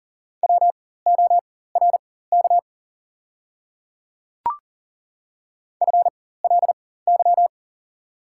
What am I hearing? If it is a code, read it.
Morse code, heard as a keyed tone, about 40 wpm, spaced as for 19 wpm, 700 Hz. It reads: WORK FLY
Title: 4x Character Spacing - Top 500 Words - 40wpm